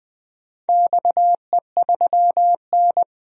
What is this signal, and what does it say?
Morse code, heard as a keyed tone, about 20 wpm, 700 Hz. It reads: XE3N
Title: DXCC-100 Call Signs - 4 Characters - 20wpm